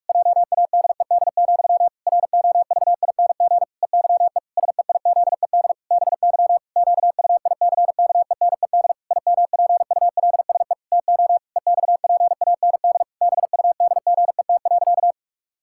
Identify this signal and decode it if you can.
Morse code, heard as a keyed tone, about 46 wpm, 700 Hz. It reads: JADED, ROVING EYE SEIZED BY QUICKENED IMPULSE TO EXPAND BUDGET.
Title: Pangram Challenge v2 - 45wpm